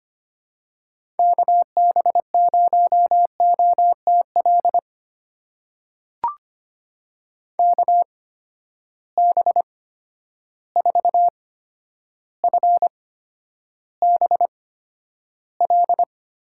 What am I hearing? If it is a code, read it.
Morse code, heard as a keyed tone, about 25 wpm, spaced as for 11 wpm, 700 Hz. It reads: KB0OTL KB4FBL
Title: USA Call Signs - 8x Character Spacing - 25wpm